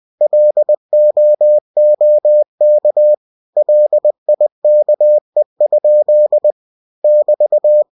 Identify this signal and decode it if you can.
Morse code, heard as a keyed tone, about 20 wpm, 600 Hz. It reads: LOOK LIKE? <BT>
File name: cw-books / winnie-the-pooh / winnie-20wpm-page-0084